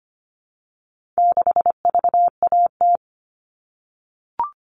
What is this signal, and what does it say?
Morse code, heard as a keyed tone, about 25 wpm, 700 Hz. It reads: 6VAT
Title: All Cut Numbers - 4-Digit 25wpm